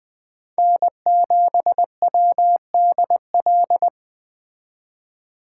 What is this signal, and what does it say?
Morse code, heard as a keyed tone, about 20 wpm, 700 Hz. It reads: N7WDL
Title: Speed Racing - USA 1x3 Call Signs 17wpm